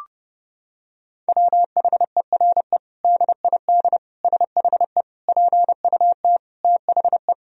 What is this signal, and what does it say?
Morse code, heard as a keyed tone, about 30 wpm, 700 Hz. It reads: WHERE DID SHE PUT THE
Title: Sentences from Top 1000 Words - Review Part 3 - 30wpm